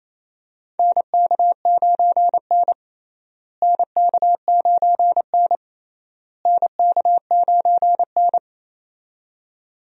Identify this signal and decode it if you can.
Morse code, heard as a keyed tone, about 28 wpm, 700 Hz. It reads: NK9N NK9N NK9N
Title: USA Call Signs - 4 Characters - Repeated 3x - 28wpm